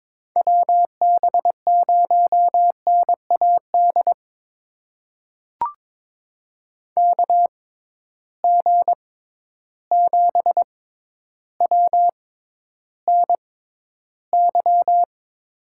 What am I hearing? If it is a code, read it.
Morse code, heard as a keyed tone, about 22 wpm, spaced as for 11 wpm, 700 Hz. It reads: WB0NAD KG7WNY
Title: USA Call Signs - 6 Characters - 6x Character Spacing - 22wpm